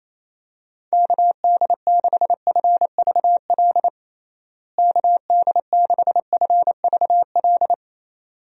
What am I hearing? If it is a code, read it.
Morse code, heard as a keyed tone, about 28 wpm, 700 Hz. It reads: KD6FVL KD6FVL
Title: USA Call Signs - 6 Characters - Repeated 2x - 28wpm